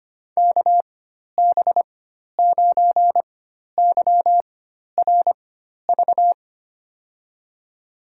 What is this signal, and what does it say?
Morse code, heard as a keyed tone, about 25 wpm, spaced as for 12 wpm, 700 Hz. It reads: KB9YRV
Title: USA Call Signs - 6 Characters - 4x Character Spacing - 25wpm